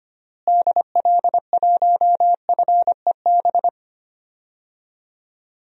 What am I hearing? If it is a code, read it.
Morse code, heard as a keyed tone, about 25 wpm, 700 Hz. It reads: DL1FEB
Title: DXCC-100 Call Signs - 6 Characters - Repeated 3x - 25wpm